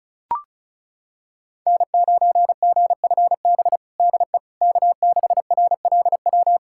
Morse code, heard as a keyed tone, about 34 wpm, 700 Hz. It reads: N9GFB DE K6RLW
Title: USA Callsign DE Callsign - 5 Characters - 35wpm